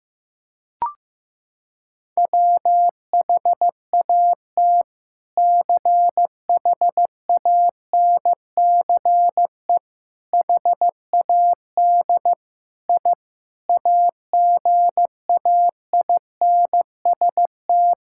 Morse code, heard as a keyed tone, about 15 wpm, 700 Hz. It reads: WHAT CHANCE HAD I AGAINST
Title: Sentences from Top 1000 Words - Review Part 6 - 15wpm